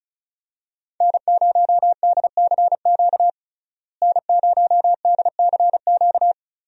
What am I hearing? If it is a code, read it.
Morse code, heard as a keyed tone, about 36 wpm, 700 Hz. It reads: N0DCQ N0DCQ
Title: USA Call Signs - 5 Characters - Repeated 3x - 35wpm